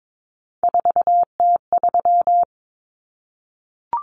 Morse code, heard as a keyed tone, about 22 wpm, 700 Hz. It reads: 4T3
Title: Common Cut Numbers - 3-Digit 22wpm